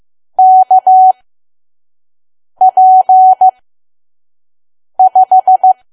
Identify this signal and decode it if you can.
Morse code, heard as a keyed tone, about 15 wpm, spaced as for 5 wpm, 750 Hz. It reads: KP5
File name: Archive / 5 WPM files / PM